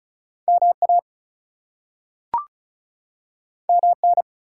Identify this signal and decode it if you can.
Morse code, heard as a keyed tone, about 36 wpm, 700 Hz. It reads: MA MN
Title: US States and Canadian Provinces 35wpm